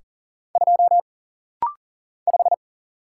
Morse code, heard as a keyed tone, about 40 wpm, 700 Hz. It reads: 2 5
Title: Single Letter-Number Alt - Rapid-Fire 40wpm